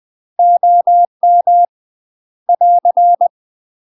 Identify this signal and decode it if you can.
Morse code, heard as a keyed tone, about 20 wpm, 700 Hz. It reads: OM <AR>
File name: UK MASTER QSOs TRAINING/20WPM